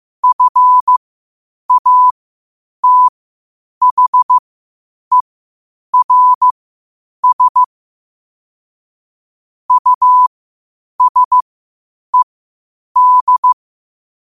Morse code, heard as a keyed tone, about 15 wpm, spaced as for 8 wpm, 1000 Hz. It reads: FATHERS USED
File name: qotdfiles / 8wpm